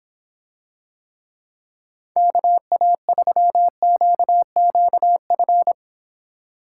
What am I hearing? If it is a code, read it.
Morse code, heard as a keyed tone, about 26 wpm, 700 Hz. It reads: KA3QQF